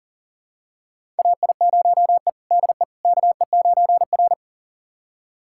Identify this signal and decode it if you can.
Morse code, heard as a keyed tone, about 40 wpm, 700 Hz. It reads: AI0E DE KE9R
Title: USA Callsign DE Callsign - 4 Characters - 40wpm